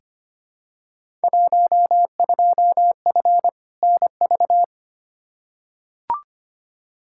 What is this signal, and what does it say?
Morse code, heard as a keyed tone, about 25 wpm, 700 Hz. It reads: 12F NV
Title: Field Day Exchanges Encore - 25wpm